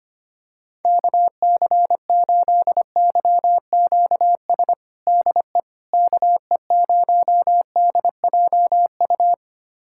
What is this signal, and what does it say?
Morse code, heard as a keyed tone, about 25 wpm, 700 Hz. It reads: KC8YQS DE KE0DJU
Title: USA Callsign DE Callsign 25wpm